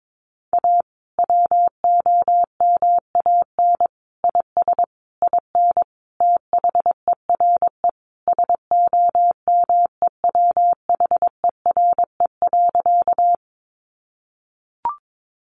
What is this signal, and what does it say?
Morse code, heard as a keyed tone, about 22 wpm, 700 Hz. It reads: A WOMAN IS IN THERE SOMEWHERE.
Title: Sentences from Top 1400 Words 22wpm